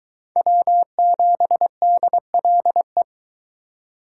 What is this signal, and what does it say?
Morse code, heard as a keyed tone, about 23 wpm, 700 Hz. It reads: W7DLE